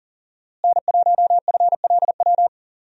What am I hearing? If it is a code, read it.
Morse code, heard as a keyed tone, about 40 wpm, 700 Hz. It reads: N1FLW
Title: USA Call Signs - 5 Characters - Repeated 3x - 40wpm